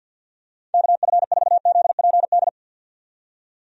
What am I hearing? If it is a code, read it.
Morse code, heard as a keyed tone, about 51 wpm, 700 Hz. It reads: KF4ZPD
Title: USA Call Signs - 6 Characters - 50wpm